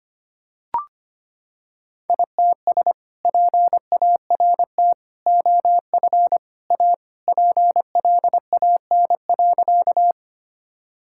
Morse code, heard as a keyed tone, about 25 wpm, 700 Hz. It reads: ITS PART OF A PLAN.
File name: practice / 25wpm / Sentences with 2-Word Phrases - 25wpm